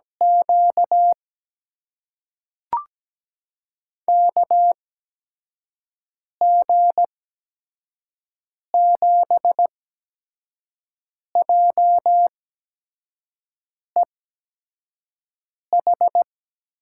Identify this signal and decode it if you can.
Morse code, heard as a keyed tone, about 17 wpm, spaced as for 5 wpm, 700 Hz. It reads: Q KG7JEH
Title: USA Call Signs - 6 Characters - 8x Character Spacing - 17wpm